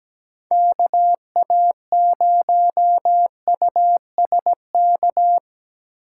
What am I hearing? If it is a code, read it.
Morse code, heard as a keyed tone, about 20 wpm, 700 Hz. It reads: KA0USK
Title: Speed Racing - USA 2x3 Call Signs 17wpm